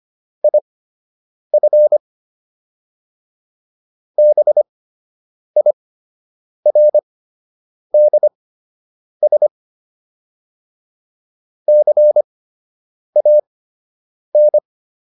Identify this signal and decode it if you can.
Morse code, heard as a keyed tone, about 25 wpm, spaced as for 8 wpm, 600 Hz. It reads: IF BIRDS CAN